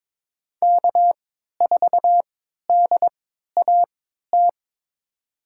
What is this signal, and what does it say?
Morse code, heard as a keyed tone, about 22 wpm, spaced as for 13 wpm, 700 Hz. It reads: K4DAT
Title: USA Call Signs - 3x Character Spacing - 22wpm